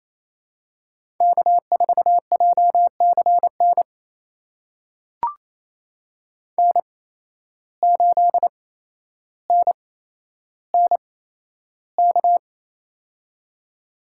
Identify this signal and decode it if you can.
Morse code, heard as a keyed tone, about 28 wpm, spaced as for 12 wpm, 700 Hz. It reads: K4JCN N8NNK
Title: USA Call Signs - 8x Character Spacing - 28wpm